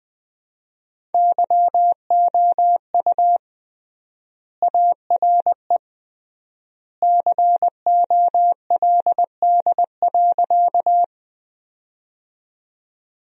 Morse code, heard as a keyed tone, about 20 wpm, 700 Hz. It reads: YOU ARE COLD.
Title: 3x Word-Spacing - Sentences from Top 400 Words - 20wpm